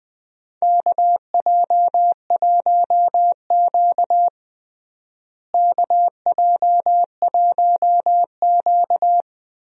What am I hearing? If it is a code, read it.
Morse code, heard as a keyed tone, about 20 wpm, 700 Hz. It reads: KJ1Q KJ1Q
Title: USA Call Signs - 4 Characters - Repeated 2x - 20wpm